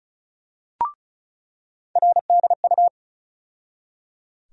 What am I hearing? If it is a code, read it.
Morse code, heard as a keyed tone, about 36 wpm, 700 Hz. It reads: RDU
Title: USA Call Sign Postfixes - 3 Letters 35wpm